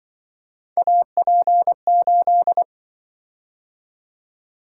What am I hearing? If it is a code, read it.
Morse code, heard as a keyed tone, about 25 wpm, 700 Hz. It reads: AP8
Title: Speed-Racing - Three Letter-Number Combinations - 20wpm